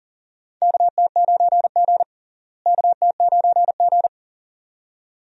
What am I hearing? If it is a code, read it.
Morse code, heard as a keyed tone, about 38 wpm, 700 Hz. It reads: KT9G KT9G
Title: USA Call Signs - 4 Characters - Repeated 2x - 40wpm